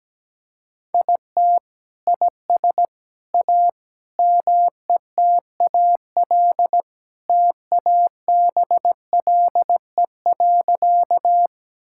Morse code, heard as a keyed tone, about 17 wpm, 700 Hz. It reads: IT IS A METAL TABLE.
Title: Sentences from Top 1000 Words - Review Part 6 - 17wpm